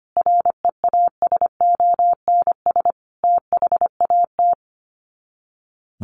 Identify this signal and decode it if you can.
Morse code, heard as a keyed tone, about 25 wpm, 700 Hz. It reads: REASONS THAT
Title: Sets of 5 Words Encore 25wpm